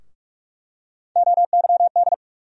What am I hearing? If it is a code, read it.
Morse code, heard as a keyed tone, about 46 wpm, 700 Hz. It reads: OYD
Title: USA Call Sign Postfixes - 3 Letters 45wpm